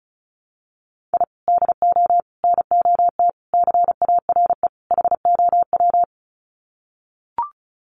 Morse code, heard as a keyed tone, about 34 wpm, 700 Hz. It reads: I DO NOT CARE HOW
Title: Sets of 5 Words Encore2 35wpm